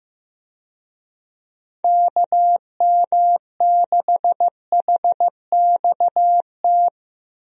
Morse code, heard as a keyed tone, about 16 wpm, 700 Hz. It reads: KM6HXT
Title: USA Call Signs - 6 Characters - Repeated 2x - 15wpm